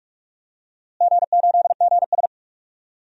Morse code, heard as a keyed tone, about 47 wpm, 700 Hz. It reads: G8GS
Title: DXCC-100 Call Signs - 4 Characters - Repeated 2x - 45wpm